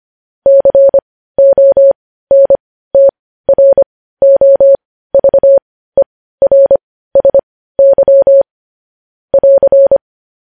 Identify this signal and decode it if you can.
Morse code, heard as a keyed tone, about 25 wpm, spaced as for 15 wpm, 550 Hz. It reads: CONTROVERSY <AR>
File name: podcast / News headlines-1x-25x15